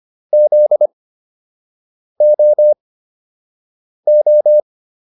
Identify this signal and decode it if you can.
Morse code, heard as a keyed tone, about 25 wpm, spaced as for 6 wpm, 600 Hz. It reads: ZOO